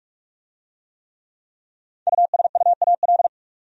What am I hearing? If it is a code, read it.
Morse code, heard as a keyed tone, about 41 wpm, 700 Hz. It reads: USUAL